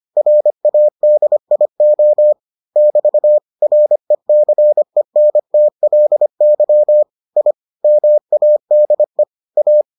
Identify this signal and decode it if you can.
Morse code, heard as a keyed tone, about 25 wpm, 600 Hz. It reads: RADIO <BT> RECENTLY I MADE A